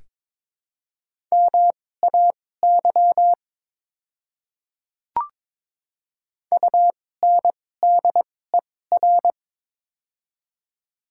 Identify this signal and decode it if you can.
Morse code, heard as a keyed tone, about 22 wpm, spaced as for 16 wpm, 700 Hz. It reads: MAY UNDER
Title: 2x Character Spacing - Top 500 Words - 22wpm